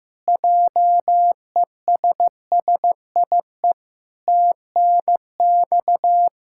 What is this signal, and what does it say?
Morse code, heard as a keyed tone, about 16 wpm, 700 Hz. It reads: JESSIE TNX
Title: Practice QSO - Exchange 3 - More Info - 15wpm